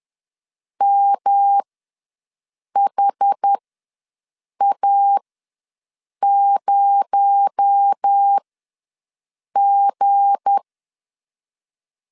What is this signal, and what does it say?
Morse code, heard as a keyed tone, about 10 wpm, spaced as for 6 wpm, 800 Hz. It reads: MHA0G
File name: LEZIONI / 7 GF9Z / 7 GF9 ws 10wpm 4034